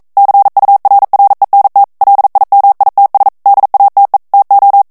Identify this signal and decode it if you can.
Morse code, heard as a keyed tone, about 42 wpm, 800 Hz. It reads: CURRENT LIMITS DATE TO